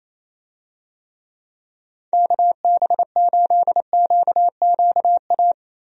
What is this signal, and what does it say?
Morse code, heard as a keyed tone, about 28 wpm, 700 Hz. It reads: KB8QQA
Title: USA Call Signs - 6 Characters - Repeated 2x - 28wpm